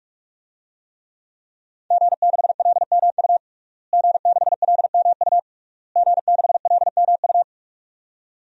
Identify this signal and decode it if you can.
Morse code, heard as a keyed tone, about 46 wpm, 700 Hz. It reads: G6LMU G6LMU G6LMU
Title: DXCC-100 Call Signs - 5 Characters - Repeated 3x - 45wpm